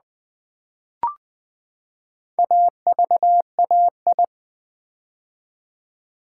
Morse code, heard as a keyed tone, about 20 wpm, 700 Hz. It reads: AVAI
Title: Build A Word - 4 Letters At A Time - 12-Letter Words - 20wpm